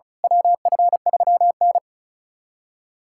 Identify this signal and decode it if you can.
Morse code, heard as a keyed tone, about 36 wpm, 700 Hz. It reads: WF3N
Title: USA Callsign DE Callsign - 4 Characters - 35wpm